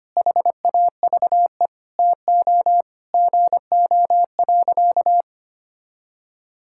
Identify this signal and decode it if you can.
Morse code, heard as a keyed tone, about 25 wpm, 700 Hz. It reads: HAVE TO GO.
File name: practice / 25wpm / Sentences with 4-Word Phrases - 25wpm